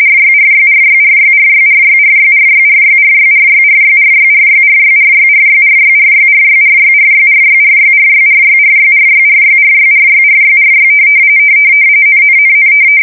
RTTY, about 45.45 baud. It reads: RYRYRYRYRYRYRYRYRYRYRYRYRYRYRYRYRYRYRYRYRYRYRYRYRYRYRYRYRYRYRYRYRY NNNN
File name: fsk170 / ry-only